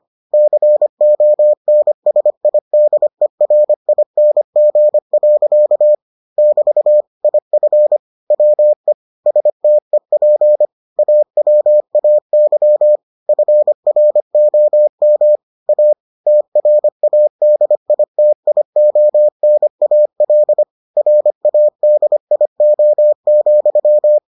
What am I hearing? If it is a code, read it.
Morse code, heard as a keyed tone, about 25 wpm, 600 Hz. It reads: CONSIDERING. <BT> IF WE STEP AWAY FROM A TRADITIONAL RADIO,